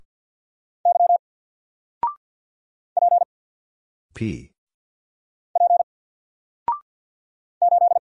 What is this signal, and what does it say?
Morse code, heard as a keyed tone, about 50 wpm, 700 Hz. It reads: Y P P 8
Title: Single Letter-Number 50wpm